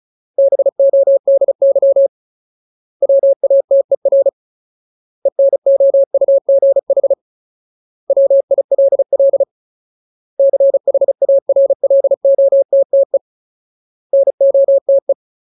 Morse code, heard as a keyed tone, about 34 wpm, 550 Hz. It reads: BODY WATER ENOUGH WILL CHARLOTTE NOTE
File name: Ultimate Word List / 35WPM Ultimate